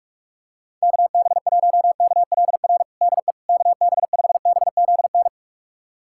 Morse code, heard as a keyed tone, about 41 wpm, 700 Hz. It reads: KB1KLR DE KB5BZN
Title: USA Callsign DE Callsign - 6 Characters - 45wpm